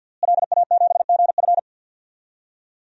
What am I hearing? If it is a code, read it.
Morse code, heard as a keyed tone, about 47 wpm, 700 Hz. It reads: LA7GF
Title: DXCC-100 Call Signs - 5 Characters - Repeated 3x - 50wpm